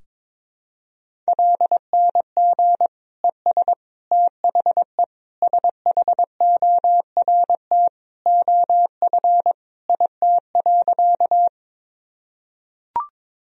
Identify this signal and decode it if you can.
Morse code, heard as a keyed tone, about 22 wpm, 700 Hz. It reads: LNG ES THE SHORT OF IT.
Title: Sentences from Top 400 Words and Abbreviations 22wpm